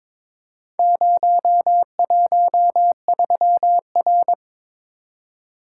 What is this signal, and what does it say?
Morse code, heard as a keyed tone, about 22 wpm, 700 Hz. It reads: 013R
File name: practice / 22wpm / Four Letter-Number Combinations - 22wpm